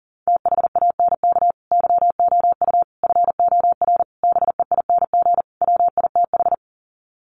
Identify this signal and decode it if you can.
Morse code, heard as a keyed tone, about 40 wpm, 700 Hz. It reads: THANK YOU FOR BEING WITH
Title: Sets of 5 Words 40wpm